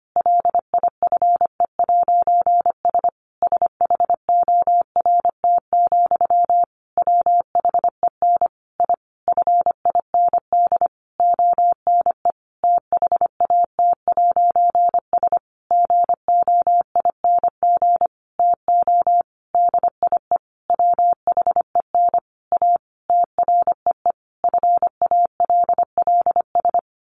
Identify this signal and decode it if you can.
Morse code, heard as a keyed tone, about 25 wpm, 700 Hz. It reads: LIFE'S SHORT, WHEN I FIND ONE THAT'S GOING TO DIE WHEN A TREE FALLS
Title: A Sound Of Thunder 25wpm